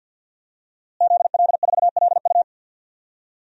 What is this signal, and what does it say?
Morse code, heard as a keyed tone, about 50 wpm, 700 Hz. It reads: ZL4LU
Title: DXCC-100 Call Signs - 5 Characters - Repeated 3x - 50wpm